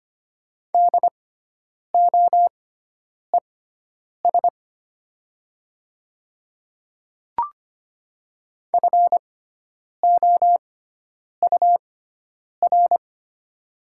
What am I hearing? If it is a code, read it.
Morse code, heard as a keyed tone, about 25 wpm, spaced as for 9 wpm, 700 Hz. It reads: DOES FOUR
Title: 6x Character Spacing - Top 300 Words - 25wpm